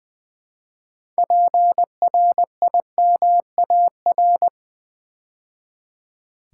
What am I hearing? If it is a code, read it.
Morse code, heard as a keyed tone, about 20 wpm, 700 Hz. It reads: PRIMAR
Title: Build A Word - 3 Letters At A Time - 9-Letter Words - 20wpm